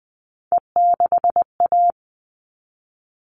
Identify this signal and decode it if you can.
Morse code, heard as a keyed tone, about 20 wpm, 700 Hz. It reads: E6A
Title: All Cut Numbers - 3-Digit 20wpm